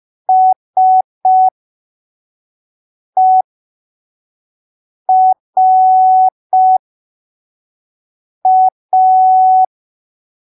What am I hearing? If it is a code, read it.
Morse code, heard as a keyed tone, about 5 wpm, spaced as for 3 wpm, 750 Hz. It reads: SERA
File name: arquivos / texto / texto 25 05ppm